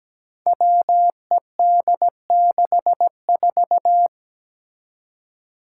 Morse code, heard as a keyed tone, about 17 wpm, 700 Hz. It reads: WED64
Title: Five Challenging Letter-Number Combinations 17wpm